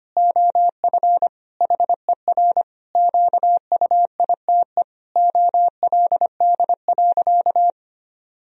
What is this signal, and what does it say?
Morse code, heard as a keyed tone, about 25 wpm, 700 Hz. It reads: OF HER QUITE OLD.
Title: Sentences from Top 1000 Words - Review Part 3 - 25wpm